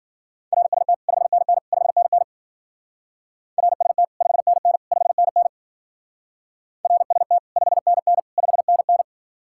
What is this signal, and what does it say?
Morse code, heard as a keyed tone, about 48 wpm, 700 Hz. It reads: RST 5NN 5NN RST 5NN 5NN RST 5NN 5NN
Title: Speed-Racing Common QSO Phrases - 40wpm